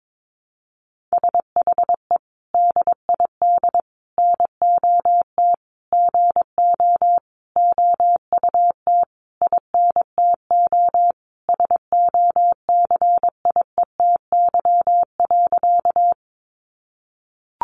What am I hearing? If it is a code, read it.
Morse code, heard as a keyed tone, about 22 wpm, 700 Hz. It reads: SHE DID NOT GO OUT INTO SOCIETY.
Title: Sentences from Top 700 Words 22wpm